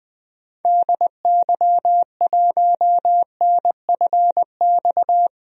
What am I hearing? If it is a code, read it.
Morse code, heard as a keyed tone, about 21 wpm, 700 Hz. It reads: DY1NFX
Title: DXCC-100 Call Signs - 6 Characters - 20wpm